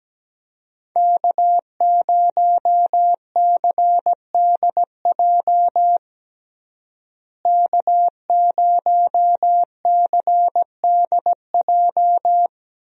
Morse code, heard as a keyed tone, about 17 wpm, 700 Hz. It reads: K0CDJ K0CDJ
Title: USA Call Signs - 5 Characters - Repeated 2x - 17wpm